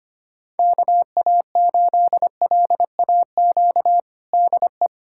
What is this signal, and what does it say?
Morse code, heard as a keyed tone, about 25 wpm, 700 Hz. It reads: KA8LAQ DE
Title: USA Callsign DE Callsign 25wpm